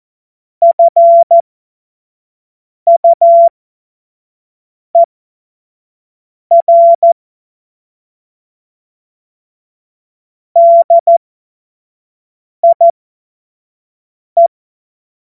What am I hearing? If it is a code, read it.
Morse code, heard as a keyed tone, about 14 wpm, spaced as for 5 wpm, 680 Hz. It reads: FUER DIE
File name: Morsekurs / Witze / Witz 30. November 2021, 25 F70